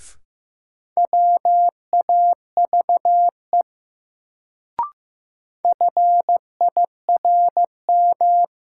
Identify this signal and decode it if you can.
Morse code, heard as a keyed tone, about 15 wpm, 700 Hz. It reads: WAVE FIRM
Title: Sentences from Top 1400 Words - New Word Drill Alt - 15wpm